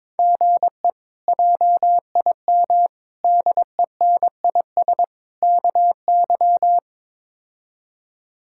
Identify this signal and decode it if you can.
Morse code, heard as a keyed tone, about 22 wpm, 700 Hz. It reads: GE JIM DENIS KY
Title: K1USN SST QSO Practice - 22wpm